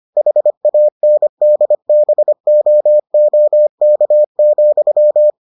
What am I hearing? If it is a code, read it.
Morse code, heard as a keyed tone, about 25 wpm, 600 Hz. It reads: HANDBOOK,